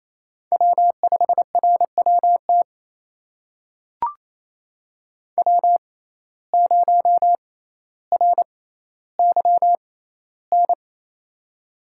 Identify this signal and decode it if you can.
Morse code, heard as a keyed tone, about 28 wpm, spaced as for 14 wpm, 700 Hz. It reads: W5RWT W0RYN
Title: USA Call Signs - 5 Characters - 6x Character Spacing - 28wpm